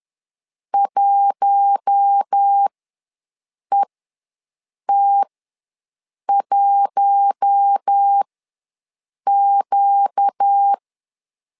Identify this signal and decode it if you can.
Morse code, heard as a keyed tone, about 10 wpm, spaced as for 6 wpm, 800 Hz. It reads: 1ET1Q
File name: LEZIONI / 4 Q1 ws 10wpm 2538